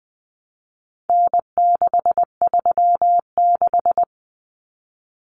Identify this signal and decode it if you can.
Morse code, heard as a keyed tone, about 20 wpm, 700 Hz. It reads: N636
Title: Common Cut Numbers - 4-Digit 20wpm